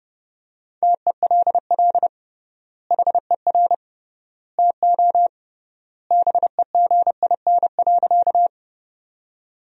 Morse code, heard as a keyed tone, about 31 wpm, 700 Hz. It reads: TELL HER TO BEGIN.
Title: 3x Word-Spacing - Sentences from Top 300 Words - 30wpm